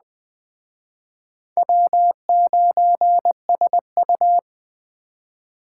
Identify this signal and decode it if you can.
Morse code, heard as a keyed tone, about 20 wpm, 700 Hz. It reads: W9SU
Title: Speed Racing - USA Call Signs 17wpm